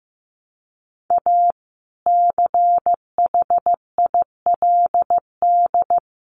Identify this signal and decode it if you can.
Morse code, heard as a keyed tone, about 15 wpm, 700 Hz. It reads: A CHILD
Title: Sentences from Top 800 Words 15wpm